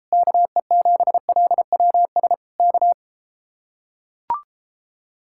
Morse code, heard as a keyed tone, about 34 wpm, 700 Hz. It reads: KE7LWS K